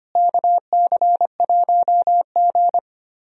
Morse code, heard as a keyed tone, about 25 wpm, 700 Hz. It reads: KC1G
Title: USA Call Signs - 4 Characters - Repeated 3x - 25wpm